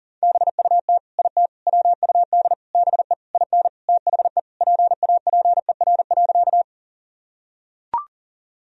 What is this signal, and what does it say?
Morse code, heard as a keyed tone, about 37 wpm, 700 Hz. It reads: BUT IT WUD BE IN THE PAPER.
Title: Sentences from Top 300 Words and Abbreviations 40wpm